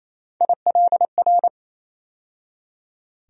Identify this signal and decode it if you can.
Morse code, heard as a keyed tone, about 28 wpm, 700 Hz. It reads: ILR
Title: USA Call Sign Postfixes - 3 Letters 28wpm